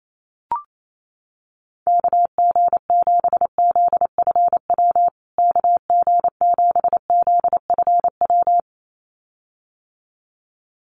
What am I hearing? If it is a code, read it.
Morse code, heard as a keyed tone, about 28 wpm, 700 Hz. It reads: KG7ZFW KG7ZFW
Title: USA Call Signs Repeated 2x 28wpm